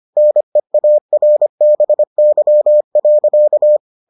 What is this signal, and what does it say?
Morse code, heard as a keyed tone, about 25 wpm, 600 Hz. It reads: NEARBY.